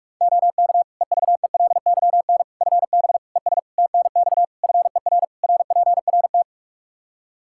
Morse code, heard as a keyed tone, about 46 wpm, 700 Hz. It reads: OK EVELYN FB ES TNX FER RPRT
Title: Practice QSO - Exchange 2 - Info - 45wpm